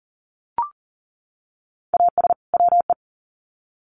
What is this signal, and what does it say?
Morse code, heard as a keyed tone, about 40 wpm, 700 Hz. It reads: AS WE
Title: Sets of 2 Words 40wpm